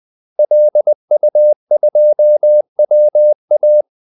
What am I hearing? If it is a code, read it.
Morse code, heard as a keyed tone, about 20 wpm, 600 Hz. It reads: LU2WA
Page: Kaldesignaler 20 wpm